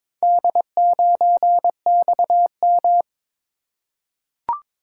Morse code, heard as a keyed tone, about 22 wpm, 700 Hz. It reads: D9XM
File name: practice / 22wpm / Four Letter-Number Combinations - 22wpm